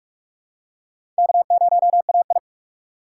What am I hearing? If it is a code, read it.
Morse code, heard as a keyed tone, about 43 wpm, 700 Hz. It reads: K0AI
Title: USA Call Signs - 4x Character Spacing - 45wpm